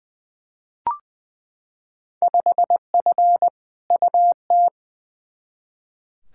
Morse code, heard as a keyed tone, about 20 wpm, 700 Hz. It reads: 5F UT
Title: Field Day Exchanges - 20wpm